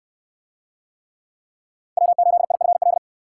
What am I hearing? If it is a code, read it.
Morse code, heard as a keyed tone, about 68 wpm, 700 Hz. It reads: W8IFL